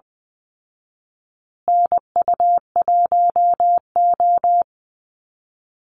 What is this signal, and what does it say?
Morse code, heard as a keyed tone, about 20 wpm, 700 Hz. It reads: NU1O
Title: Call Signs Repeated 2x - 20wpm